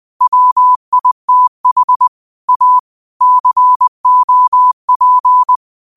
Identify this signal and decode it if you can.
Morse code, heard as a keyed tone, about 20 wpm, 1000 Hz. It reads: WITH A COP
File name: qotdfiles / 20wpm